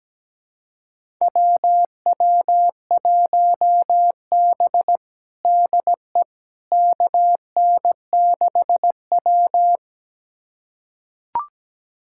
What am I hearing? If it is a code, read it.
Morse code, heard as a keyed tone, about 17 wpm, 700 Hz. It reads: WW1B DE KN6W
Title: USA Callsign DE Callsign - 4 Characters - 17wpm